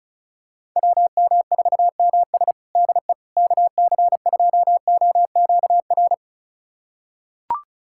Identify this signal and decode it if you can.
Morse code, heard as a keyed tone, about 36 wpm, 700 Hz. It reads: WM4MS DE KC2OQR
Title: USA Callsign DE Callsign 35wpm